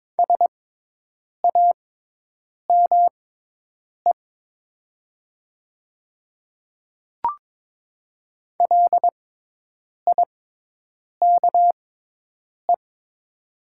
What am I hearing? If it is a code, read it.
Morse code, heard as a keyed tone, about 22 wpm, spaced as for 8 wpm, 700 Hz. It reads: SAME LIKE
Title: 6x Character Spacing - Top 200 Words - 22wpm